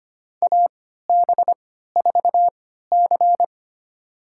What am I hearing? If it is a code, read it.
Morse code, heard as a keyed tone, about 25 wpm, spaced as for 14 wpm, 700 Hz. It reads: AB4C
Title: USA Call Signs - 3x Character Spacing - 25wpm